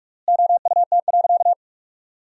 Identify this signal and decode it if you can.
Morse code, heard as a keyed tone, about 41 wpm, 700 Hz. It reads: OUT.
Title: 3x Word-Spacing - Sentences from Top 500 Words - 45wpm